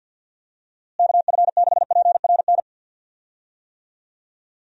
Morse code, heard as a keyed tone, about 45 wpm, 700 Hz. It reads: KF6PRN
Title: USA Call Signs - 6 Characters - Repeated 2x - 50wpm